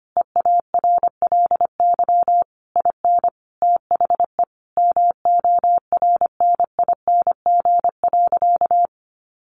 Morse code, heard as a keyed tone, about 25 wpm, 700 Hz. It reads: EARLY IN THE MORNING.
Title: Sentences from Top 1400 Words 25wpm